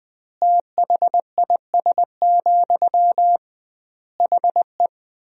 Morse code, heard as a keyed tone, about 20 wpm, 700 Hz. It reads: THIS, HE
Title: 2x Word-Spacing - Sentences from Top 400 Words - 20wpm